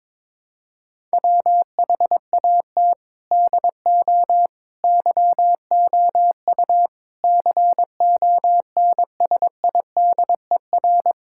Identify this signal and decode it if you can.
Morse code, heard as a keyed tone, about 22 wpm, 700 Hz. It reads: WHAT DO YOU CONSIDER
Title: Sentences from Top 1000 Words - Review Part 5 - 22wpm